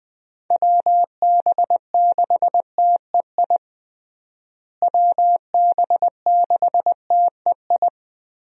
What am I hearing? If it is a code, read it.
Morse code, heard as a keyed tone, about 20 wpm, 700 Hz. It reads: WB6TEI WB6TEI
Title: USA Call Signs - 6 Characters - Repeated 2x - 20wpm